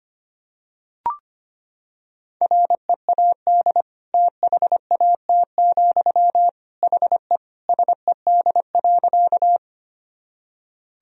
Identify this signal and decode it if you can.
Morse code, heard as a keyed tone, about 25 wpm, 700 Hz. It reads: READ THAT, HE SED.
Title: Sentences from Top 200 Words and Abbreviations 25wpm